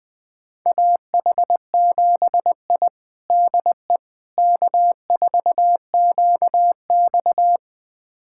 Morse code, heard as a keyed tone, about 20 wpm, 700 Hz. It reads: AH7I DE K4QX
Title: USA Callsign DE Callsign - 4 Characters - 20wpm